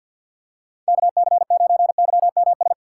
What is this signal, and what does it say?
Morse code, heard as a keyed tone, about 47 wpm, 700 Hz. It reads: KC9YMS
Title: USA Call Signs - 2x Character Spacing - 50wpm